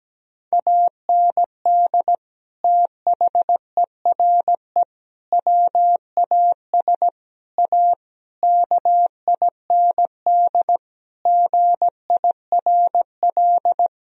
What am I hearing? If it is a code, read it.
Morse code, heard as a keyed tone, about 17 wpm, 700 Hz. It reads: AND THERE WAS A KIND GIRL
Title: Sentences from Top 1000 Words - Review Part 3 - 17wpm